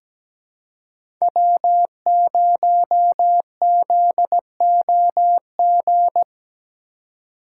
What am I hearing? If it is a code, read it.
Morse code, heard as a keyed tone, about 17 wpm, 700 Hz. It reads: W0ZOG
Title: USA Call Signs - 5 Characters - 4x Character Spacing - 17wpm